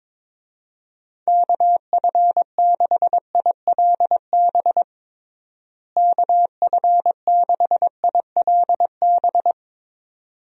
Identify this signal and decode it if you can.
Morse code, heard as a keyed tone, about 21 wpm, 700 Hz. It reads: KF6ILB KF6ILB
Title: USA Call Signs - 6 Characters - Repeated 2x - 22wpm